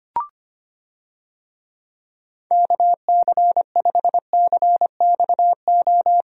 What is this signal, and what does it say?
Morse code, heard as a keyed tone, about 25 wpm, 700 Hz. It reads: KC5CXO
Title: USA Call Signs - 6 Characters - Repeated 2x - 25wpm